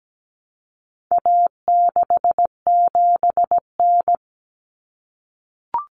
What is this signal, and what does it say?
Morse code, heard as a keyed tone, about 17 wpm, 700 Hz. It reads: A67N
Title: Common Cut Numbers - 4-Digit 17wpm